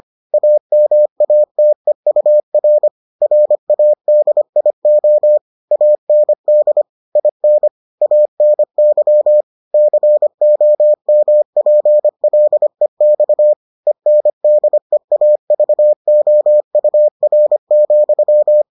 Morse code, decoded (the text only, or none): AMATEUR RADIO AND IN ANY COMPLEX ENDEAVOUR,